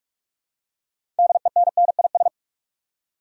Morse code, heard as a keyed tone, about 45 wpm, 700 Hz. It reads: DENNIS
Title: USA Common Names - 45wpm